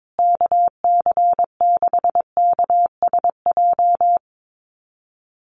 Morse code, heard as a keyed tone, about 22 wpm, 700 Hz. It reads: KC6KSJ
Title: USA Call Signs Repeated 3x 22wpm